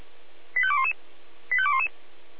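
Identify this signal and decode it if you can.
Five-tone selective call (ZVEI), heard as quick runs of five tones, tones of about 70 ms. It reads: 85211 85211
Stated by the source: ZVEI five-tone selective call received via SDR.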